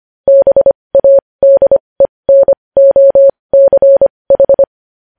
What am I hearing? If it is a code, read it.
Morse code, heard as a keyed tone, about 25 wpm, spaced as for 20 wpm, 550 Hz. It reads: BADENOCH
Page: news-headlines-in-morse-code-at-20-wpm